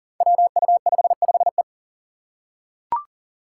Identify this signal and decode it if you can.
Morse code, heard as a keyed tone, about 37 wpm, 700 Hz. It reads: WU55E
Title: Five Challenging Letter-Number Combinations 40wpm